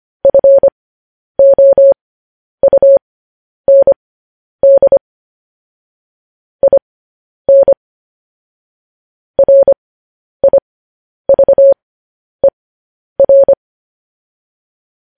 Morse code, heard as a keyed tone, about 25 wpm, spaced as for 10 wpm, 550 Hz. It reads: FOUND IN RIVER
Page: News Headlines in Morse Code at 10 WPM